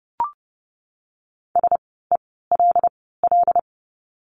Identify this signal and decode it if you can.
Morse code, heard as a keyed tone, about 30 wpm, spaced as for 17 wpm, 700 Hz. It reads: SELL
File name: Lesson 17f - Letter u - Words 30wpm